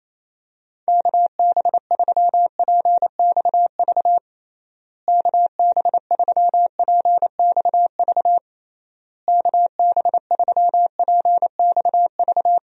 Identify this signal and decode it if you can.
Morse code, heard as a keyed tone, about 28 wpm, 700 Hz. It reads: KB3PXV KB3PXV KB3PXV
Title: USA Call Signs - 6 Characters - Repeated 3x - 28wpm